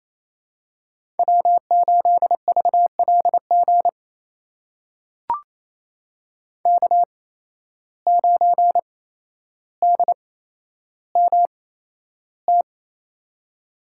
Morse code, heard as a keyed tone, about 28 wpm, spaced as for 12 wpm, 700 Hz. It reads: W8VLG K9DMT
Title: USA Call Signs - 5 Characters - 8x Character Spacing - 28wpm